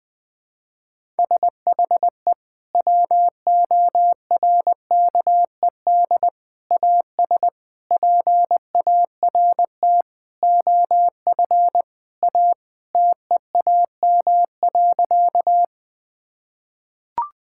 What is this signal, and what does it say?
Morse code, heard as a keyed tone, about 21 wpm, 700 Hz. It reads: SHE WORKED AS PART OF A TEAM.
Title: Sentences with 4-Word Phrases - 20wpm